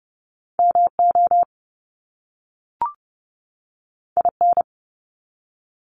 Morse code, heard as a keyed tone, about 30 wpm, 700 Hz. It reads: MO IN
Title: US States 30wpm